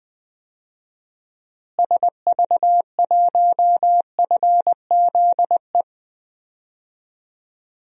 Morse code, heard as a keyed tone, about 20 wpm, 700 Hz. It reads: SV1FZE
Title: DXCC-100 Call Signs - 6 Characters - Repeated 3x - 20wpm